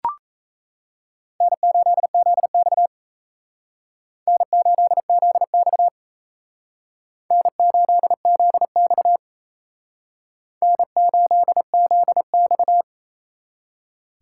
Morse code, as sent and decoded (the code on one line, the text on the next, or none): -. ---.. --.. -..- / -. ---.. --.. -..- / -. ---.. --.. -..- / -. ---.. --.. -..-
N8ZX N8ZX N8ZX N8ZX